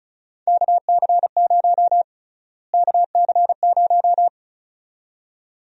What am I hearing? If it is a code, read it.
Morse code, heard as a keyed tone, about 36 wpm, 700 Hz. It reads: KC0 KC0
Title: USA Call Sign Prefixes - Repeated 2x - 35wpm